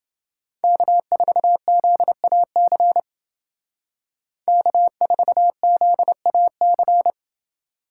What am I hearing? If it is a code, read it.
Morse code, heard as a keyed tone, about 29 wpm, 700 Hz. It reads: K4ZAC K4ZAC